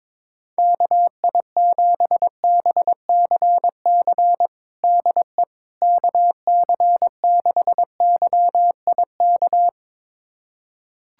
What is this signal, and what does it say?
Morse code, heard as a keyed tone, about 22 wpm, 700 Hz. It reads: KI7BCC DE KC6YIK
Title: USA Callsign DE Callsign 22wpm